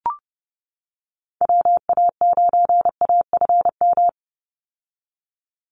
Morse code, heard as a keyed tone, about 30 wpm, 700 Hz. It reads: WA9AFM
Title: USA Call Signs 30wpm